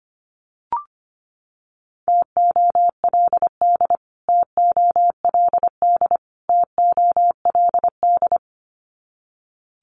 Morse code, heard as a keyed tone, about 25 wpm, 700 Hz. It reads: TOLD TOLD TOLD
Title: Top 500 Words Repeated 25wpm